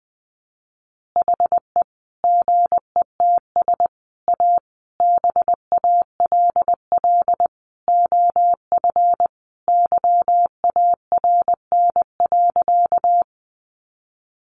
Morse code, heard as a keyed tone, about 20 wpm, 700 Hz. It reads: HE GETS A BALL OF YARN.
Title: Additional Easy Phrases 20wpm